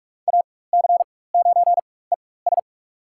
Morse code, heard as a keyed tone, about 44 wpm, spaced as for 21 wpm, 700 Hz. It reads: AC9ES
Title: USA Call Signs - 5 Characters - 4x Character Spacing - 45wpm